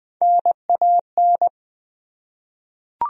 Morse code, heard as a keyed tone, about 20 wpm, 700 Hz. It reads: NAN
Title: USA Common Names - Max 3 Letters - 20wpm